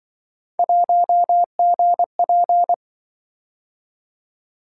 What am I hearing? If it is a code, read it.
Morse code, heard as a keyed tone, about 24 wpm, 700 Hz. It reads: 1GP
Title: Speed-Racing - Three Letter-Number Combinations - 20wpm